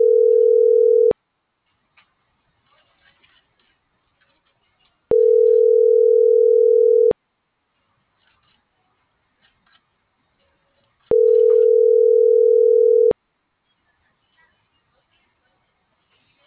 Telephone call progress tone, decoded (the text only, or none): ringback tone